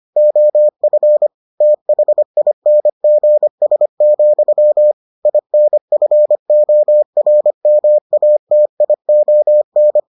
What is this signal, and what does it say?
Morse code, heard as a keyed tone, about 25 wpm, 600 Hz. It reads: OF THINGS, INFORMATION